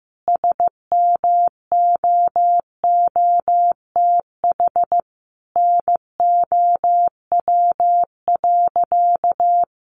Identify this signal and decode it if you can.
Morse code, heard as a keyed tone, about 15 wpm, 700 Hz. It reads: SMOOTH NOW.
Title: Sentences from Top 1400 Words 15wpm